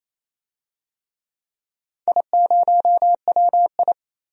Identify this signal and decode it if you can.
Morse code, heard as a keyed tone, about 28 wpm, 700 Hz. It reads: I0WI
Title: DXCC-100 Call Signs - 4 Characters - 28wpm